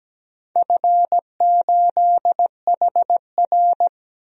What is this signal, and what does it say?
Morse code, heard as a keyed tone, about 17 wpm, 700 Hz. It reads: F8HR
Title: DXCC-100 Call Signs - 4 Characters - Repeated 3x - 17wpm